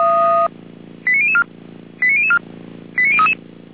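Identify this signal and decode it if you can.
Five-tone selective call (ZVEI), heard as quick runs of five tones, tones of about 70 ms. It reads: 89004 89004 89033
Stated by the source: ZVEI five-tone call 89011 received by SDR.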